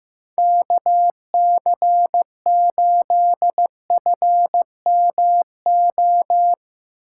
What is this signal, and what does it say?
Morse code, heard as a keyed tone, about 16 wpm, 700 Hz. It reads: KC8FMO
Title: Speed Racing - USA 2x3 Call Signs 15wpm